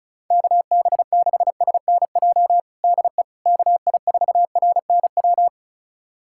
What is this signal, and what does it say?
Morse code, heard as a keyed tone, about 34 wpm, 700 Hz. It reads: KB6SNJ DE KI4RNW
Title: USA Callsign DE Callsign - 6 Characters - 35wpm